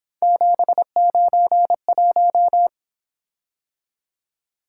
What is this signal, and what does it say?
Morse code, heard as a keyed tone, about 26 wpm, 700 Hz. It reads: 791
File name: Speed-Racing - Three Letter-Number Combinations - 22wpm